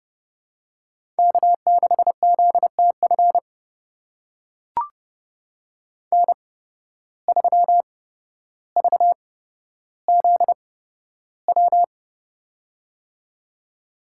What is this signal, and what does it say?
Morse code, heard as a keyed tone, about 30 wpm, spaced as for 13 wpm, 700 Hz. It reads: K6ZTF N3VZW
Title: USA Call Signs - 5 Characters - 8x Character Spacing - 30wpm